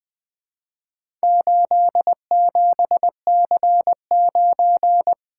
Morse code, heard as a keyed tone, about 20 wpm, 700 Hz. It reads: 87C9
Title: Four Letter-Number Combinations - 20wpm